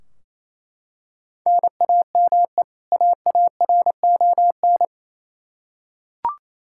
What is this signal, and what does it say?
Morse code, heard as a keyed tone, about 28 wpm, 700 Hz. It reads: NAME AARON
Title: Practice QSO - Exchange 1 - Beginning - 28wpm